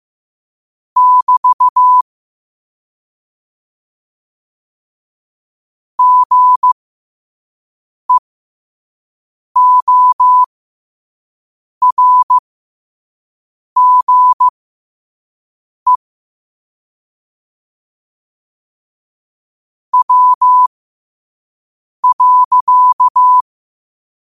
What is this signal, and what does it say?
Morse code, heard as a keyed tone, about 15 wpm, spaced as for 5 wpm, 1000 Hz. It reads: <BT> GEORGE W.
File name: qotdfiles / 5wpm